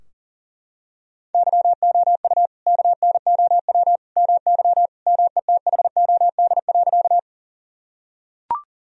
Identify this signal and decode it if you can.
Morse code, heard as a keyed tone, about 40 wpm, 700 Hz. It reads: YOU KNOW MY METHOD.
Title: Sentences from Top 1000 Words - Review Part 3 - 40wpm